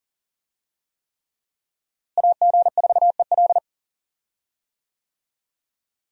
Morse code, heard as a keyed tone, about 40 wpm, 690 Hz. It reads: AG4EL
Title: USA Call Signs - 5 Characters - Repeated 3x - 40wpm